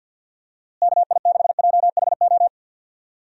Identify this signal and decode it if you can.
Morse code, heard as a keyed tone, about 48 wpm, 700 Hz. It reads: KI6JHO
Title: USA Call Signs - 6 Characters - Repeated 2x - 50wpm